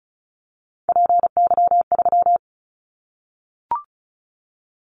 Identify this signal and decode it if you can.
Morse code, heard as a keyed tone, about 34 wpm, 700 Hz. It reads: PY3
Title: Callsign Prefixes for Top 100 DXCC - 35wpm